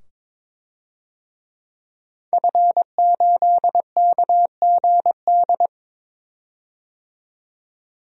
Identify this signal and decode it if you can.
Morse code, heard as a keyed tone, about 22 wpm, 700 Hz. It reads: F8KGD
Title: DXCC-100 Call Signs - 5 Characters - 22wpm